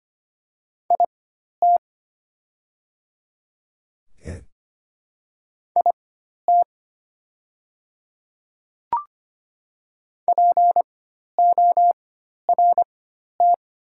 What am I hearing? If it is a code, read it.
Morse code, heard as a keyed tone, about 25 wpm, spaced as for 12 wpm, 690 Hz. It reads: IT IT PORT
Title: 4x Character Spacing - Top 500 Words - 25wpm